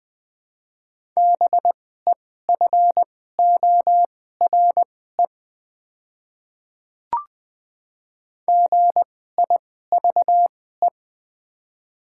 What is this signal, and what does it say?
Morse code, heard as a keyed tone, about 20 wpm, spaced as for 14 wpm, 700 Hz. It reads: BEFORE GIVE
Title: 2x Character Spacing - Top 200 Words - 20wpm